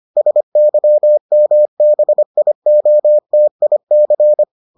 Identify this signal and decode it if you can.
Morse code, heard as a keyed tone, about 25 wpm, 600 Hz. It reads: SYMBIOTIC